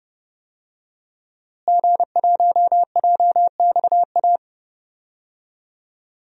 Morse code, heard as a keyed tone, about 30 wpm, 700 Hz. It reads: G1JXA
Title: DXCC-100 Call Signs - 5 Characters - Repeated 2x - 30wpm